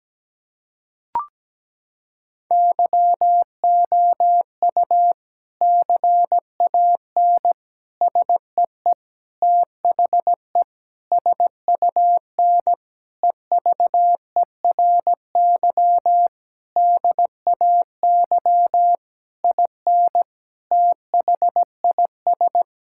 Morse code, heard as a keyed tone, about 17 wpm, 700 Hz. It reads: YOU CAN SEE THE SUN EVERY DAY IN THIS
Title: Sentences from Top 1600 Words 17wpm